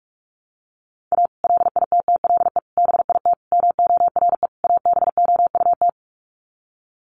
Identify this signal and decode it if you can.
Morse code, heard as a keyed tone, about 44 wpm, 700 Hz. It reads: A LITTLE BIT MORE ABOUT
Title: Sets of 5 Words Encore2 45wpm